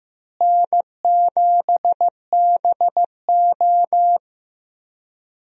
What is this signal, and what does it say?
Morse code, heard as a keyed tone, about 15 wpm, 700 Hz. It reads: N7BO
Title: USA Call Signs - 4 Characters - Repeated 2x - 15wpm